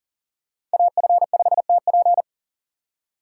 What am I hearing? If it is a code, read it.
Morse code, heard as a keyed tone, about 40 wpm, 700 Hz. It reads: AF5TP
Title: USA Call Signs - 5 Characters - 3x Character Spacing - 40wpm